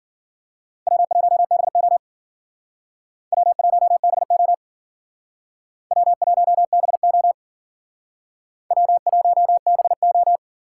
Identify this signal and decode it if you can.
Morse code, heard as a keyed tone, about 51 wpm, 700 Hz. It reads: W1BO W1BO W1BO W1BO